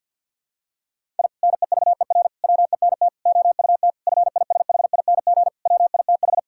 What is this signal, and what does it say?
Morse code, heard as a keyed tone, about 51 wpm, 700 Hz. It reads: I NEVER WENT OUT FISHING WITH
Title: Sentences from Top 1000 Words - Review Part 5 - 50wpm